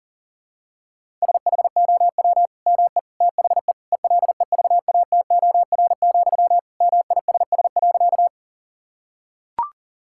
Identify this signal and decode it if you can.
Morse code, heard as a keyed tone, about 40 wpm, 700 Hz. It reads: SHOW ME THE ELEVATOR, MISS.
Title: Sentences from Top 1600 Words 40wpm